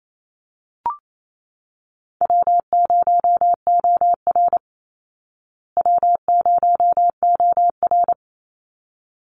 Morse code, heard as a keyed tone, about 28 wpm, 700 Hz. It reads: W0OR W0OR
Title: Call Signs Repeated 3x - 28wpm